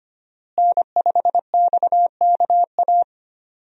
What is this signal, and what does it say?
Morse code, heard as a keyed tone, about 25 wpm, 700 Hz. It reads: N5XKA
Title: USA Call Signs - 5 Characters - Repeated 2x - 25wpm